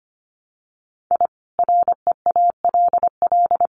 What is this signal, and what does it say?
Morse code, heard as a keyed tone, about 25 wpm, 700 Hz. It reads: I REALL
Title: Sets of 4 Words 25wpm